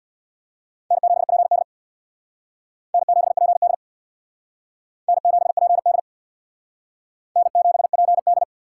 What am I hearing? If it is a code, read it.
Morse code, heard as a keyed tone, about 61 wpm, 700 Hz. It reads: N7PD N7PD N7PD N7PD
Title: Speed Racing - USA 1x2 Call Signs 50wpm